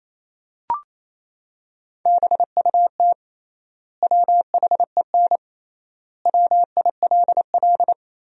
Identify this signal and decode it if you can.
Morse code, heard as a keyed tone, about 28 wpm, 700 Hz. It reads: BUT WHEN WILL
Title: 3x Word-Spacing - Sentences from Top 100 Words - 28wpm